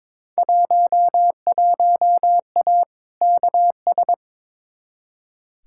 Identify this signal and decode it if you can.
Morse code, heard as a keyed tone, about 22 wpm, 700 Hz. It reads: 11A KS
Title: Field Day Exchanges Encore - 22wpm